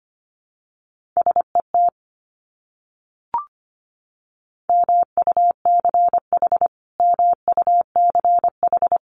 Morse code, heard as a keyed tone, about 25 wpm, 700 Hz. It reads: SET MUCH MUCH
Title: Top 200 Words Repeated 25wpm